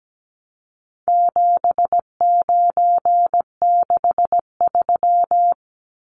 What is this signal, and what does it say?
Morse code, heard as a keyed tone, about 17 wpm, 700 Hz. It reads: 7963
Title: Numbers - 4-Digit 17wpm